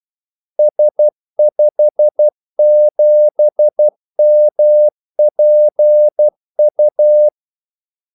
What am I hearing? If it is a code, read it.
Morse code, heard as a keyed tone, about 12 wpm, 600 Hz. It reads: S57MPU
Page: 2 ekstra ordmellemrum mellem kaldesignalerne